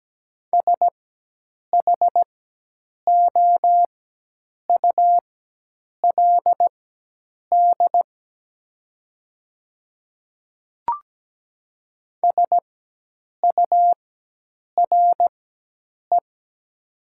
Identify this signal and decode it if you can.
Morse code, heard as a keyed tone, about 17 wpm, spaced as for 8 wpm, 700 Hz. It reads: SHOULD SURE
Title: 4x Character Spacing - Top 300 Words - 17wpm